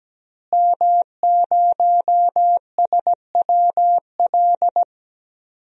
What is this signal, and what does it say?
Morse code, heard as a keyed tone, about 17 wpm, 700 Hz. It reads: M0SWL
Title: DXCC-100 Call Signs - 5 Characters - Repeated 3x - 17wpm